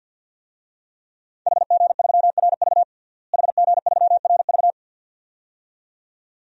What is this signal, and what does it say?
Morse code, heard as a keyed tone, about 51 wpm, 700 Hz. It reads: HG3RV HG3RV
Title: DXCC-100 Call Signs - 5 Characters - Repeated 2x - 50wpm